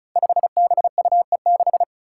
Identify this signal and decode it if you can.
Morse code, heard as a keyed tone, about 36 wpm, 700 Hz. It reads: 5BUE6
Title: Five Challenging Letter-Number Combinations 35wpm